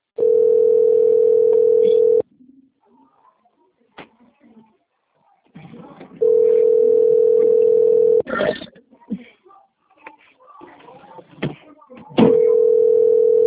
Ringback tone (North America), 440 + 480 Hz.